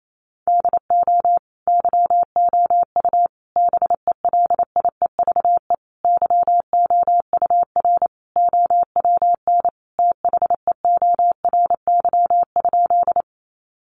Morse code, heard as a keyed tone, about 28 wpm, 700 Hz. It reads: DO YOU BELIEVE YOUR OWN THEORY?
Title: Sentences from Top 1000 Words 28wpm